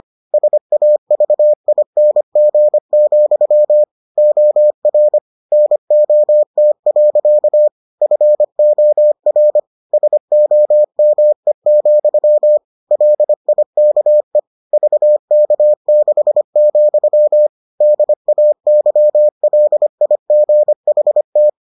Morse code, heard as a keyed tone, about 25 wpm, 600 Hz. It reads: SAVING, OR NOT. FOR SOME, LIKE VK6, DAYLIGHT